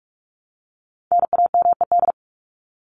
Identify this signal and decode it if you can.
Morse code, heard as a keyed tone, about 44 wpm, 700 Hz. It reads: NAMED
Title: ED Suffix - Top 100 Words with Base Words 45wpm